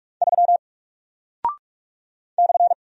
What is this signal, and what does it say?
Morse code, heard as a keyed tone, about 46 wpm, 700 Hz. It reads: 3 /
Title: Single Character (Letter:Number:Punctuation:Prosign) 45wpm